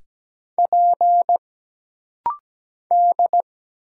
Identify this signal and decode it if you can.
Morse code, heard as a keyed tone, about 17 wpm, 700 Hz. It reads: P D
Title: Single Letters - Rapid-Fire 17wpm